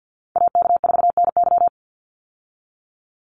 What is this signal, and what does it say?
Morse code, heard as a keyed tone, about 50 wpm, 700 Hz. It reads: AK4NY
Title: Call Signs Repeated 2x - 50wpm